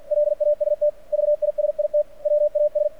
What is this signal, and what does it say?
Morse code, heard as a keyed tone, about 50 wpm, 600 Hz. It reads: LAST VISIT WAS
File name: archived / CRSnet